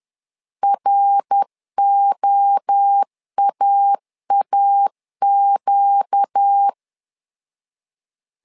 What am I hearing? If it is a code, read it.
Morse code, heard as a keyed tone, about 10 wpm, 800 Hz. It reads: ROAAQ
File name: LEZIONI / 4 Q 10wpm 2210